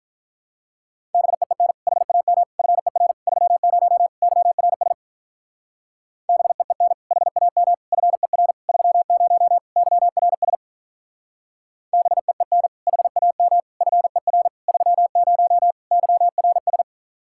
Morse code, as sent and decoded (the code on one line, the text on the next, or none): -... . -. / .... .- -- / ..-. .-. / ...-- ----- / -.-- .-. ... / -... . -. / .... .- -- / ..-. .-. / ...-- ----- / -.-- .-. ... / -... . -. / .... .- -- / ..-. .-. / ...-- ----- / -.-- .-. ...
BEEN HAM FER 30 YRS BEEN HAM FER 30 YRS BEEN HAM FER 30 YRS